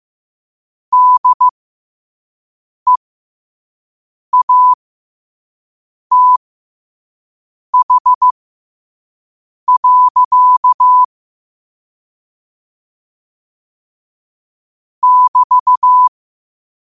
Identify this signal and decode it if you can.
Morse code, heard as a keyed tone, about 15 wpm, spaced as for 5 wpm, 1000 Hz. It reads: DEATH. <BT>
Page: Quotes for Tue, 12 Aug 2025 in Morse Code at 5 words per minute.